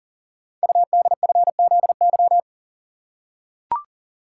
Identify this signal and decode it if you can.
Morse code, heard as a keyed tone, about 42 wpm, 700 Hz. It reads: UDFZY
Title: Code Groups - 5 Letters - 40wpm